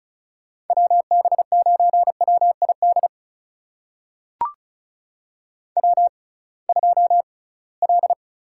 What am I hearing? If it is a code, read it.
Morse code, heard as a keyed tone, about 36 wpm, spaced as for 23 wpm, 700 Hz. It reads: WB9WID W2L
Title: USA Call Signs - 6x Character Spacing - 35wpm